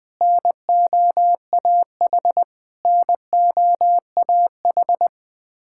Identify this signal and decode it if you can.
Morse code, heard as a keyed tone, about 21 wpm, 700 Hz. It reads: NOAH NOAH
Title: Practice QSO - Exchange 1 - Beginning - 20wpm